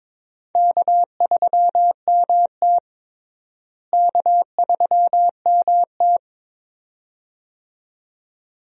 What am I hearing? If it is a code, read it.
Morse code, heard as a keyed tone, about 22 wpm, 700 Hz. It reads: K3MT K3MT
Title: USA Call Signs - 4 Characters - Repeated 2x - 22wpm